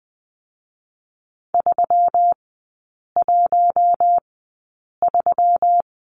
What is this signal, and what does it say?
Morse code, heard as a keyed tone, about 20 wpm, spaced as for 8 wpm, 700 Hz. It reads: 313
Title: 121 Lesson 20e - Number 3 - Three Characters 20wpm